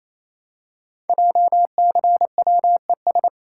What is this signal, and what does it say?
Morse code, heard as a keyed tone, about 29 wpm, 700 Hz. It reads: JCWES
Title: Code Groups - 5 Letters - 28wpm